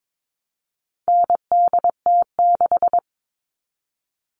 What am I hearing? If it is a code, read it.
Morse code, heard as a keyed tone, about 22 wpm, 700 Hz. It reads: NDT6
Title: All Cut Numbers - 4-Digit 22wpm